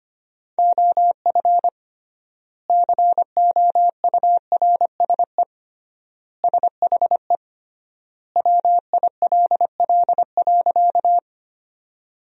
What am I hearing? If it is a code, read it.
Morse code, heard as a keyed tone, about 25 wpm, 700 Hz. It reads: OF COURSE SHE WILL.
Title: 3x Word-Spacing - Sentences from Top 500 Words - 25wpm